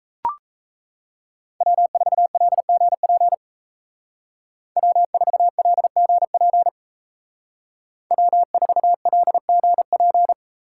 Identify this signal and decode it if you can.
Morse code, heard as a keyed tone, about 36 wpm, 700 Hz. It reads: W4LGP W4LGP W4LGP